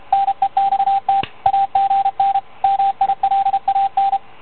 Morse code, heard as a keyed tone, about 33 wpm, 790 Hz, with radio noise. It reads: NEXT AGN MILAN